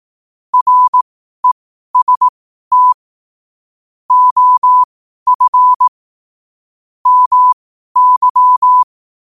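Morse code, heard as a keyed tone, about 18 wpm, spaced as for 12 wpm, 1000 Hz. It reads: REST OF MY
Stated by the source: Morse code at 12 wpm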